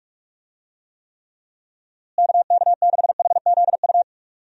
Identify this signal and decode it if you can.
Morse code, heard as a keyed tone, about 44 wpm, 700 Hz. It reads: KK6HZU
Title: USA Call Signs - 6 Characters - Repeated 3x - 45wpm